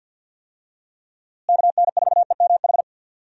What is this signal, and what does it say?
Morse code, heard as a keyed tone, about 50 wpm, 700 Hz. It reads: KN4EMH
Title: USA Call Signs - 6 Characters - Repeated 3x - 50wpm